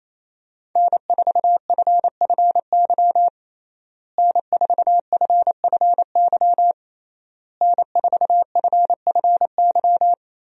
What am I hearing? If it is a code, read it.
Morse code, heard as a keyed tone, about 28 wpm, 700 Hz. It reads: N4FFY N4FFY N4FFY
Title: USA Call Signs - 5 Characters - Repeated 3x - 28wpm